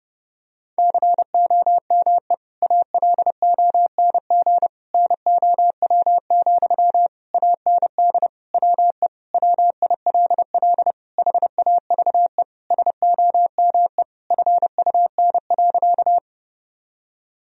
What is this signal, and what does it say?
Morse code, heard as a keyed tone, about 30 wpm, 700 Hz. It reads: COME ALONG NOW, AND WE WILL HAVE SOME FUN.
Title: Sentences from Top 1000 Words - Review Part 4 - 30wpm